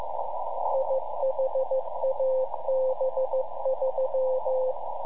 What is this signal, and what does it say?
Morse code, heard as a keyed tone, about 15 wpm, 520 Hz, with radio noise. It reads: IHAB3
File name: iHAB-3 / Repeater Recordings